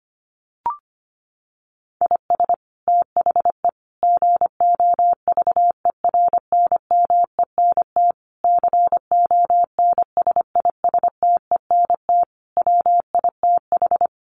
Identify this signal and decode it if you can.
Morse code, heard as a keyed tone, about 25 wpm, 700 Hz. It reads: IS THE GOVERNMENT CONSISTENT WITH
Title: Sentences from Top 1500 Words 25wpm